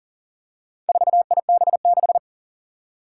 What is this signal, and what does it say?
Morse code, heard as a keyed tone, about 39 wpm, 700 Hz. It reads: 4IB6
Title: Four Challenging Letter-Number Combinations 40wpm